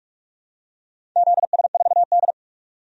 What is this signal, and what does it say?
Morse code, heard as a keyed tone, about 44 wpm, 700 Hz. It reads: ZS4D
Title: DXCC-100 Call Signs - 4 Characters - 45wpm